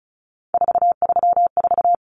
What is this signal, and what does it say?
Morse code, heard as a keyed tone, about 34 wpm, 700 Hz. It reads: 434
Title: Common Cut Numbers - 3-Digit 35wpm